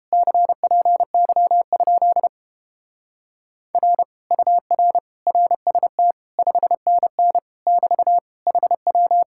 Morse code, heard as a keyed tone, about 29 wpm, 700 Hz. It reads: CPY? R UR RST 5NN <BT> HW